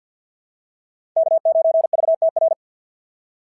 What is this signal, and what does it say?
Morse code, heard as a keyed tone, about 50 wpm, 690 Hz. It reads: K9VTR
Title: SOTA QSO Practice 50wpm